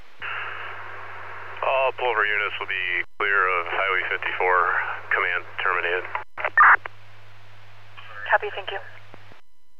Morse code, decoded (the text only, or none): none